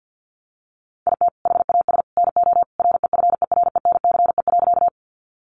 Morse code, heard as a keyed tone, about 50 wpm, 700 Hz. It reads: IT HAS NO REFERENCE.
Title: Sentences from Top 1200 Words 50wpm